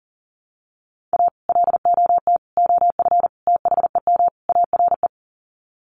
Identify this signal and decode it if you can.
Morse code, heard as a keyed tone, about 40 wpm, 700 Hz. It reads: A LOT OF THEM ARE
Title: Sets of 5 Words Encore2 40wpm